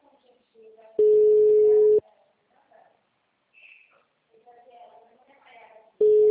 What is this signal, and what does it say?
Ringback tone (Europe, 425 Hz), with speech.
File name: out- -2037- - - .35616